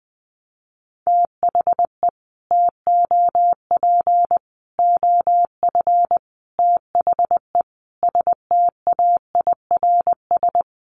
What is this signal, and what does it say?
Morse code, heard as a keyed tone, about 20 wpm, 700 Hz. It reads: THE TOP OF THE STAIRS
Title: Sets of 5 Words Encore 20wpm